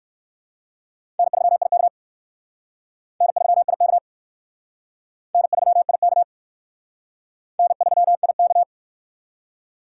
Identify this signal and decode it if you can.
Morse code, heard as a keyed tone, about 54 wpm, 700 Hz. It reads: N3IK N3IK N3IK N3IK
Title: Speed Racing - USA 1x2 Call Signs 45wpm